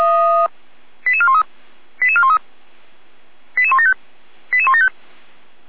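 Five-tone selective call (ZVEI), heard as quick runs of five tones, tones of about 70 ms. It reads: 88413 88413 88175 88175
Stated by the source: ZVEI five-tone selective call received by SDR.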